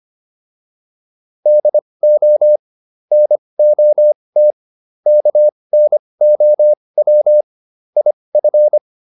Morse code, heard as a keyed tone, about 25 wpm, spaced as for 20 wpm, 600 Hz. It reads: DO NOT KNOW IF